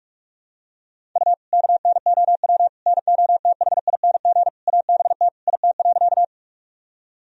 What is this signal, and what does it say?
Morse code, heard as a keyed tone, about 46 wpm, 700 Hz. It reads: U KNOW NOTHING ABT IT.
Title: Sentences from Top 500 Words and Abbreviations 45wpm